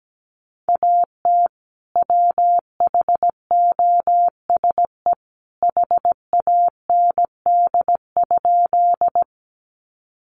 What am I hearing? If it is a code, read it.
Morse code, heard as a keyed tone, about 17 wpm, 700 Hz. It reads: AT WHOSE HAND?
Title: Sentences from Top 1200 Words 17wpm